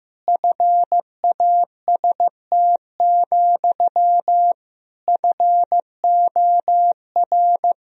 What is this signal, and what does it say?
Morse code, heard as a keyed tone, about 16 wpm, 700 Hz. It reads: FAST, FOR
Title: Sentences from Top 1000 Words - Review Part 2 - 15wpm